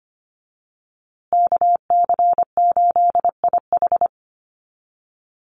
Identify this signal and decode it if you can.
Morse code, heard as a keyed tone, about 25 wpm, 700 Hz. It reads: KC8IH
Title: USA Call Signs Repeated 2x 25wpm